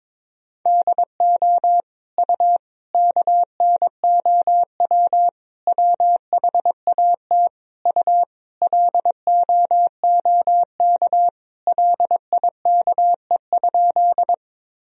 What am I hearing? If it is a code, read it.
Morse code, heard as a keyed tone, about 22 wpm, 700 Hz. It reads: DO U KNOW WHAT U LOOK LIKE?
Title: Sentences from Top 100 Words and Abbreviations 22wpm